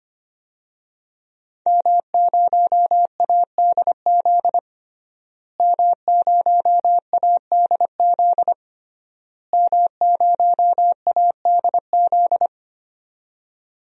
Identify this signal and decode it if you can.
Morse code, heard as a keyed tone, about 25 wpm, 700 Hz. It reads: M0ADZ M0ADZ M0ADZ
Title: DXCC-100 Call Signs - 5 Characters - Repeated 3x - 25wpm